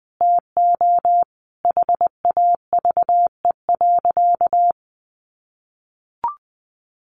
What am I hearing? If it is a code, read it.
Morse code, heard as a keyed tone, about 19 wpm, 700 Hz. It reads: TO HAVE.
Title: Sentences from Top 1200 Words 20wpm